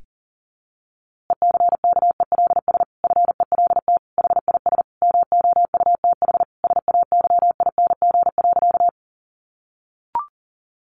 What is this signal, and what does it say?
Morse code, heard as a keyed tone, about 37 wpm, 700 Hz. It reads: ECKELS FELT HIS MOUTH SAYING.
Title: A Sound Of Thunder 40wpm